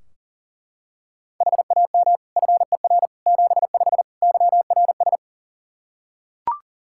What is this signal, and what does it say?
Morse code, heard as a keyed tone, about 40 wpm, 700 Hz. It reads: HAM FER 75 YRS
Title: Practice QSO - Exchange 3 - More Info - 40wpm